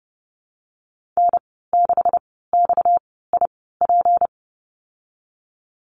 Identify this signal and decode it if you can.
Morse code, heard as a keyed tone, about 32 wpm, spaced as for 17 wpm, 700 Hz. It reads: N6XIP
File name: Lesson 37g - Letter x - Call Signs 30wpm